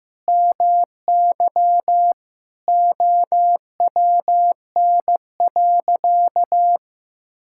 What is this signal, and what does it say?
Morse code, heard as a keyed tone, about 15 wpm, 700 Hz. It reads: MY OWN.
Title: Sentences from Top 1000 Words - Review Part 6 - 15wpm